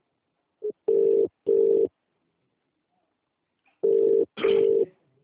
Ringback tone (UK), 400 + 450 Hz.